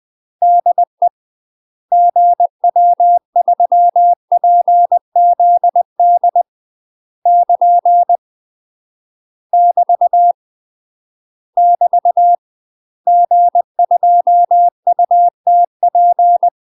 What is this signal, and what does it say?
Morse code, heard as a keyed tone, about 20 wpm, 700 Hz. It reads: DE GW3PZD <KN> <BT> <BT> G2UTP
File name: UK MASTER QSOs TRAINING / 20WPM / UK MASTER QSOs 0038